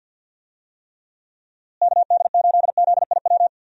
Morse code, heard as a keyed tone, about 48 wpm, 700 Hz. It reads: KD8ZIW
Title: USA Call Signs - 6 Characters - Repeated 2x - 50wpm